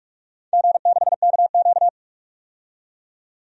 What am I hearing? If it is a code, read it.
Morse code, heard as a keyed tone, about 44 wpm, 700 Hz. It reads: G6KQ
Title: DXCC-100 Call Signs - 4 Characters - 45wpm